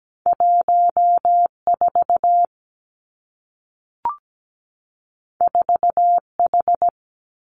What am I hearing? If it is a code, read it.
Morse code, heard as a keyed tone, about 17 wpm, 700 Hz. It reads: 14 4H
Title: Two Challenging Letter-Number Combinations 17wpm